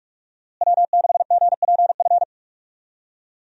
Morse code, heard as a keyed tone, about 46 wpm, 700 Hz. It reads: W6GPF
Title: USA Call Signs - 5 Characters - Repeated 3x - 45wpm